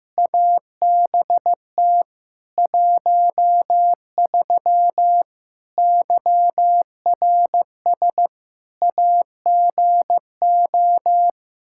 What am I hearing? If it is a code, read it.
Morse code, heard as a keyed tone, about 15 wpm, 700 Hz. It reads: ABT 13 YRS AGO
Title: Practice QSO - Exchange 3 - More Info - 15wpm